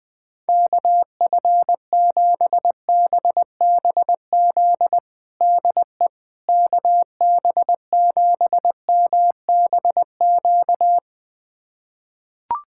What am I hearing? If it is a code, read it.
Morse code, heard as a keyed tone, about 20 wpm, 700 Hz. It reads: KF7BBZ DE KB7MBQ
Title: USA Callsign DE Callsign - 6 Characters - 20wpm